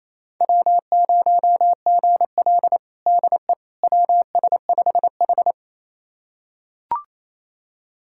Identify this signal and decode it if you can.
Morse code, heard as a keyed tone, about 28 wpm, 700 Hz. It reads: W0GL DE WS5H
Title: USA Callsign DE Callsign - 4 Characters - 28wpm